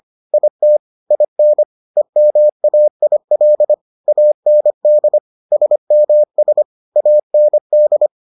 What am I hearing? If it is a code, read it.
Morse code, heard as a keyed tone, about 25 wpm, 600 Hz. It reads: IT IN EMAIL AND SMS AND